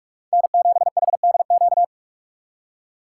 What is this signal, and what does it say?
Morse code, heard as a keyed tone, about 46 wpm, 690 Hz. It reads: N7HDQ
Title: USA Call Signs - 5 Characters - 6x Character Spacing - 45wpm